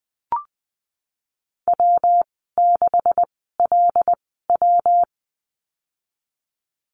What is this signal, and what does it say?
Morse code, heard as a keyed tone, about 20 wpm, spaced as for 15 wpm, 700 Hz. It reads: W6LW
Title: 173 Lesson 28g - Number 9 - Call Signs 20wpm